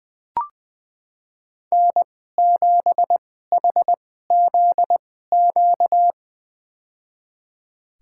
Morse code, heard as a keyed tone, about 20 wpm, spaced as for 14 wpm, 700 Hz. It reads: N7HZQ
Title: USA Call Signs - 5 Characters - 2x Character Spacing - 20wpm